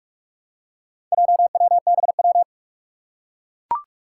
Morse code, heard as a keyed tone, about 44 wpm, 700 Hz. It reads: JWBW
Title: Four Challenging Letter-Number Combinations 45wpm